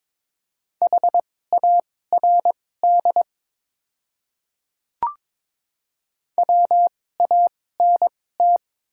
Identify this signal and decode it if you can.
Morse code, heard as a keyed tone, about 22 wpm, spaced as for 16 wpm, 700 Hz. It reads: HARD WANT
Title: 2x Character Spacing - Top 300 Words - 22wpm